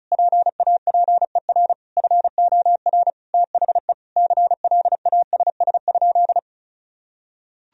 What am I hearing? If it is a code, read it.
Morse code, heard as a keyed tone, about 36 wpm, 700 Hz. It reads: PAPER FOR THE CLASS?
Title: Sentences from Top 1000 Words - Review Part 2 - 35wpm